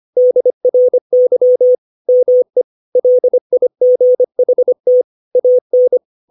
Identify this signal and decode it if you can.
Morse code, heard as a keyed tone, about 25 wpm, 500 Hz. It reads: DRY ME LIGHT AN
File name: 632 words / 632 Words 25WPM